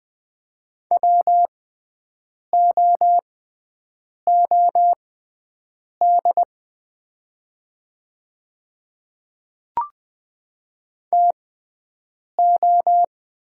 Morse code, heard as a keyed tone, about 20 wpm, spaced as for 7 wpm, 700 Hz. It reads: WOOD TO
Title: 6x Character Spacing - Top 300 Words - 20wpm